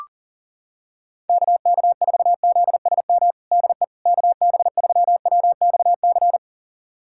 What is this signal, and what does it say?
Morse code, heard as a keyed tone, about 40 wpm, 700 Hz. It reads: KK4ZSM DE KB3WXC
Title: USA Callsign DE Callsign - 6 Characters - 40wpm